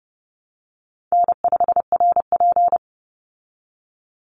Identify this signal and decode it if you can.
Morse code, heard as a keyed tone, about 29 wpm, 700 Hz. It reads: N5RP
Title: Call Signs Repeated 3x - 30wpm